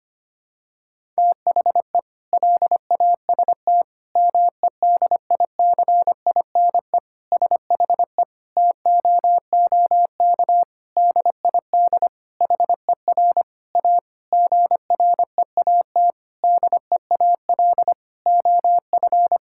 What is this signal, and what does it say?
Morse code, heard as a keyed tone, about 25 wpm, 700 Hz. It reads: THE LAST MEDICINE SHE TOOK DID HER A GREAT DEAL OF
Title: Sentences from Top 1000 Words - Review Part 6 - 25wpm